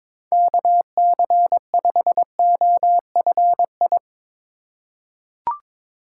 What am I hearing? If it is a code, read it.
Morse code, heard as a keyed tone, about 22 wpm, 700 Hz. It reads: KC5OFI
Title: USA Call Signs - 6 Characters - 2x Character Spacing - 22wpm